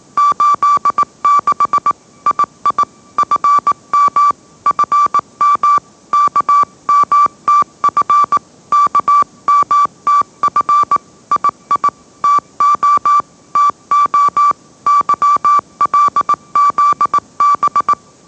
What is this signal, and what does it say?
Morse code, heard as a keyed tone, about 18 wpm, 1160 Hz. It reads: 86 II FM FM KMTF KMTF II TO TO YLZB
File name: mrx / mrx082 20wpm